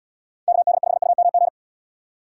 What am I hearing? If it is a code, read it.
Morse code, heard as a keyed tone, about 69 wpm, 700 Hz. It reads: KD5UMG